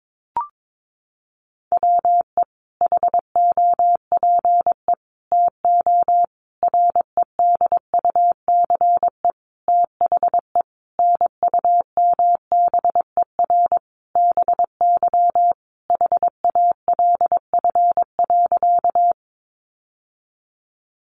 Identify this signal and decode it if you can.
Morse code, heard as a keyed tone, about 22 wpm, 700 Hz. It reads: WE HOPE TO REDUCE THE NUMBER BY HALF.
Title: Sentences from Top 1100 Words 22wpm